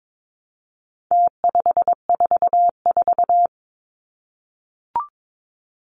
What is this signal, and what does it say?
Morse code, heard as a keyed tone, about 22 wpm, 700 Hz. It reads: T544
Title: Numbers - Contest Style - 4-Digit 22wpm